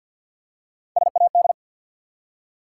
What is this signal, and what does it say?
Morse code, heard as a keyed tone, about 48 wpm, 700 Hz. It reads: SAD
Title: Three Letter Words - 50wpm